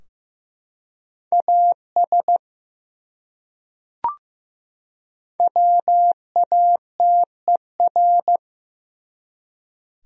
Morse code, heard as a keyed tone, about 15 wpm, 700 Hz. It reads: AS WATER
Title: Top 100 Words 15wpm